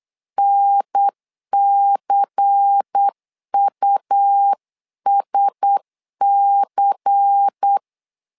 Morse code, heard as a keyed tone, about 8 wpm, 800 Hz. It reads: NCUSC
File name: LEZIONI / 5 LBVC / 5 LBVC 8wpm 2935